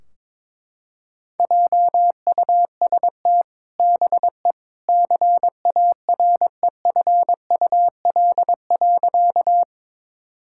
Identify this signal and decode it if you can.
Morse code, heard as a keyed tone, about 22 wpm, 700 Hz. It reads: JUST BE CAREFUL.
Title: Sentences from Top 1600 Words 22wpm